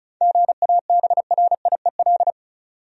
Morse code, heard as a keyed tone, about 36 wpm, 700 Hz. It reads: GABRIEL
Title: USA Common Names - 35wpm